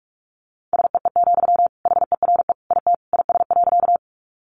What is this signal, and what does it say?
Morse code, heard as a keyed tone, about 44 wpm, 700 Hz. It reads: SEE, HERE IT IS.
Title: Sentences from Top 200 Words 45wpm